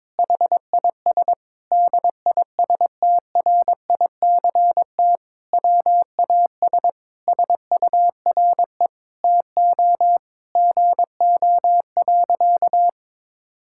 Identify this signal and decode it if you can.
Morse code, heard as a keyed tone, about 22 wpm, 700 Hz. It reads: HIS DISTRICT WAS SURE TO GO.
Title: Sentences from Top 1600 Words 22wpm